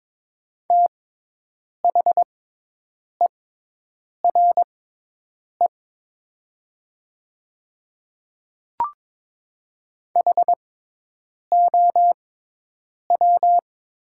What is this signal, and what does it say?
Morse code, heard as a keyed tone, about 22 wpm, spaced as for 8 wpm, 700 Hz. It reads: THERE HOW
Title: 6x Character Spacing - Top 100 Words - 22wpm